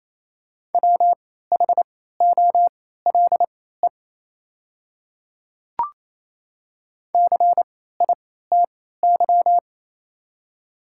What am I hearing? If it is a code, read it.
Morse code, heard as a keyed tone, about 28 wpm, spaced as for 16 wpm, 690 Hz. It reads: WHOLE CITY
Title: 3x Character Spacing - Top 500 Words - 28wpm